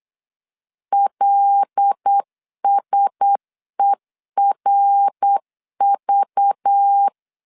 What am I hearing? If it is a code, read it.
Morse code, heard as a keyed tone, about 8 wpm, 800 Hz. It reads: LSERV
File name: LEZIONI / 6 PKX0 / 6 P 8wpm 3025